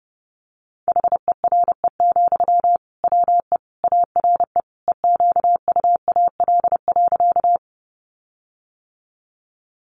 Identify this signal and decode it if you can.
Morse code, heard as a keyed tone, about 30 wpm, 700 Hz. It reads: HERE, WE ARE EQUAL.
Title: Sentences from Top 1400 Words 30wpm